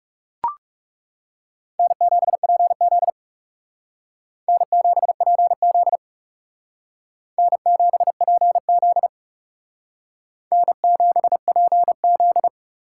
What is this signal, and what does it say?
Morse code, heard as a keyed tone, about 38 wpm, 700 Hz. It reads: N7PZ N7PZ N7PZ N7PZ